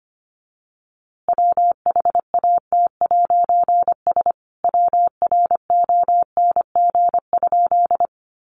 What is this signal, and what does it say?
Morse code, heard as a keyed tone, about 25 wpm, 700 Hz. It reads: WHAT'S WRONG?
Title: A Sound Of Thunder 25wpm